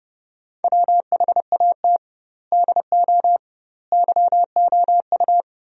Morse code, heard as a keyed tone, about 32 wpm, 700 Hz. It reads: WHAT DO YOU
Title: 2x Word-Spacing - Sentences from Top 100 Words - 30wpm